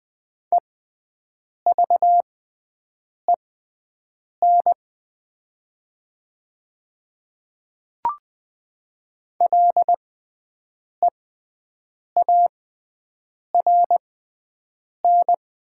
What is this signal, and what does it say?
Morse code, heard as a keyed tone, about 20 wpm, spaced as for 7 wpm, 700 Hz. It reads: EVEN LEARN
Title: 6x Character Spacing - Top 500 Words - 20wpm